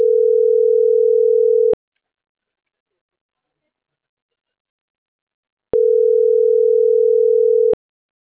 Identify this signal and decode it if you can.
Ringback tone (North America), 440 + 480 Hz.